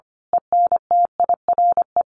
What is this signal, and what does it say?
Morse code, heard as a keyed tone, about 25 wpm, 700 Hz. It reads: ENTIRE
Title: A Sound Of Thunder 25wpm